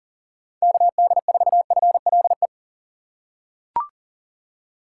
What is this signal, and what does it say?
Morse code, heard as a keyed tone, about 40 wpm, 700 Hz. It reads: KD4FLE
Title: USA Call Signs - 6 Characters - 3x Character Spacing - 40wpm